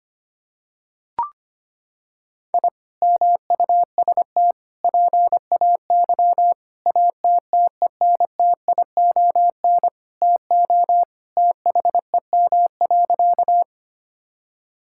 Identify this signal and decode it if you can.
Morse code, heard as a keyed tone, about 25 wpm, 700 Hz. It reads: I MUST PAY ATTENTION TO THEM.
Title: Sentences from Top 1000 Words - Review Part 6 - 25wpm